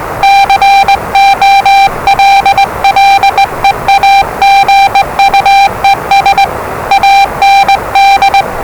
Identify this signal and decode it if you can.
Morse code, heard as a keyed tone, about 20 wpm, 790 Hz, with noise. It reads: COLLEAGUES AND
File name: WellingtonRadioZlwCloseDownOnHfAndMf / 01-Zlw 4277khz 417.5 qrt until 0925z